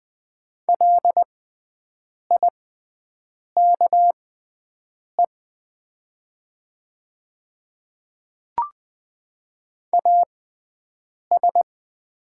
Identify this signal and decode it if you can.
Morse code, heard as a keyed tone, about 20 wpm, spaced as for 7 wpm, 700 Hz. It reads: LIKE AS
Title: 6x Character Spacing - Top 100 Words - 20wpm